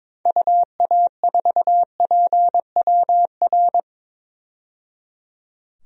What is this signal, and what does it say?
Morse code, heard as a keyed tone, about 22 wpm, 700 Hz. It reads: UA4PWR
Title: DXCC-100 Call Signs - 6 Characters - 22wpm